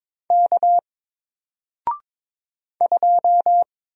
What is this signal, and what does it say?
Morse code, heard as a keyed tone, about 22 wpm, 700 Hz. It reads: K 2
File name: Single-Letter-Number-22wpm